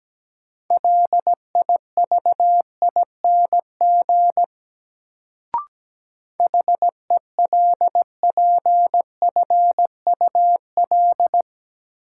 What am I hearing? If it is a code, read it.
Morse code, heard as a keyed tone, about 17 wpm, 700 Hz. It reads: LIVING HELPFUL
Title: Sentences from Top 1000 Words - Review Part 6 - New Word Drill - 17wpm